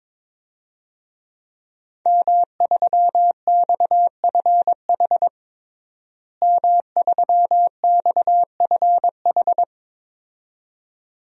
Morse code, heard as a keyed tone, about 21 wpm, 700 Hz. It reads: M3XFH M3XFH
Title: DXCC-100 Call Signs - 5 Characters - Repeated 2x - 22wpm